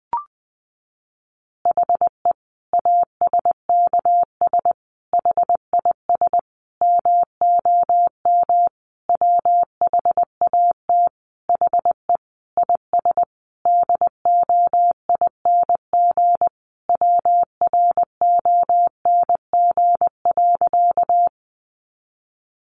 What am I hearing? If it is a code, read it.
Morse code, heard as a keyed tone, about 20 wpm, 700 Hz. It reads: HE ASKS HIS MOM WHAT HE IS DOING WRONG.
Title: Easy Phrases 20wpm